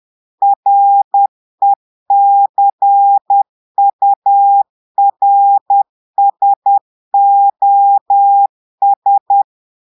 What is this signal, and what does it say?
Morse code, heard as a keyed tone, about 10 wpm, 800 Hz. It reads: RECURSOS
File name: portugais texte en clair 2 vitesse 10 mots minutes